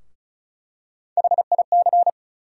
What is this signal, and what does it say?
Morse code, heard as a keyed tone, about 34 wpm, 700 Hz. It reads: HIC
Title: USA Call Sign Postfixes - 3 Letters 35wpm